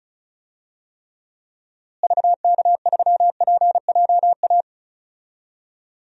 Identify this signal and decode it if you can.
Morse code, heard as a keyed tone, about 34 wpm, 700 Hz. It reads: VK3PJA
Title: DXCC-100 Call Signs - 6 Characters - Repeated 2x - 35wpm